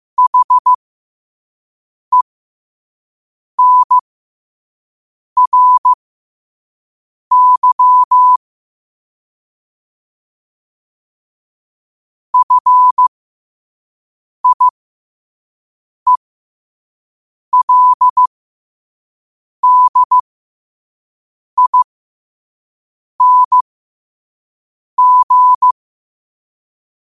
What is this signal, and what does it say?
Morse code, heard as a keyed tone, about 15 wpm, spaced as for 5 wpm, 1000 Hz. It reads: HENRY FIELDING
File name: qotdfiles / 5wpm